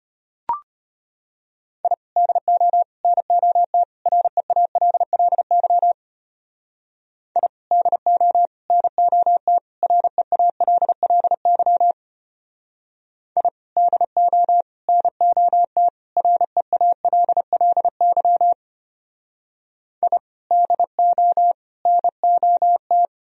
Morse code, heard as a keyed tone, about 32 wpm, 700 Hz. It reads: I DO NOT REALLY I DO NOT REALLY I DO NOT REALLY I DO NOT